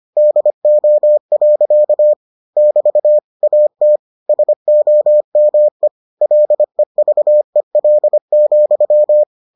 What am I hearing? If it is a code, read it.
Morse code, heard as a keyed tone, about 25 wpm, 600 Hz. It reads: DO. <BT> AT SOME LEVEL,